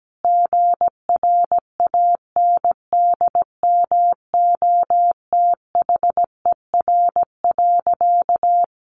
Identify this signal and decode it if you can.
Morse code, heard as a keyed tone, about 17 wpm, 700 Hz. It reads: GRANDMOTHER.
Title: Sentences from Top 1200 Words 17wpm